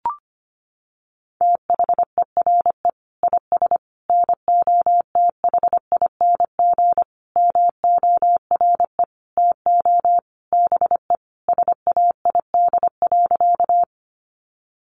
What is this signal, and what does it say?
Morse code, heard as a keyed tone, about 25 wpm, 700 Hz. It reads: THERE IS NOTHING MORE TO BE SAID.
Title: Sentences from Top 500 Words 25wpm